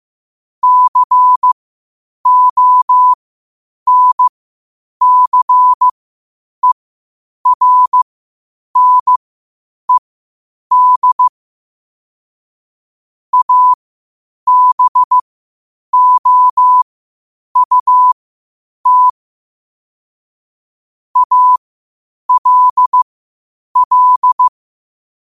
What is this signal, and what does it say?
Morse code, heard as a keyed tone, about 15 wpm, spaced as for 8 wpm, 1000 Hz. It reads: CONCERNED ABOUT ALL
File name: qotdfiles / 8wpm